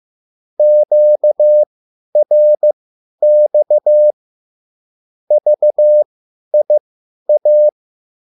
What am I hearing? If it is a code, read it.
Morse code, heard as a keyed tone, about 15 wpm, spaced as for 10 wpm, 600 Hz. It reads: QRX VIA
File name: QSL / CW lesson